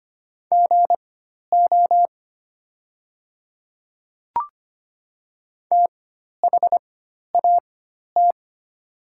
Morse code, heard as a keyed tone, about 25 wpm, spaced as for 12 wpm, 700 Hz. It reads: GO THAT
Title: 4x Character Spacing - Top 200 Words - 25wpm